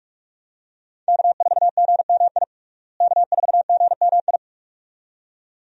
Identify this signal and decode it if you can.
Morse code, heard as a keyed tone, about 42 wpm, 700 Hz. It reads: K4GMI K4GMI
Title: USA Call Signs - 5 Characters - Repeated 2x - 45wpm